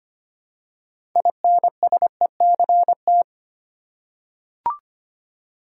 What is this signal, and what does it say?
Morse code, heard as a keyed tone, about 25 wpm, 700 Hz. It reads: INSECT
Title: Six Letter Words - 25wpm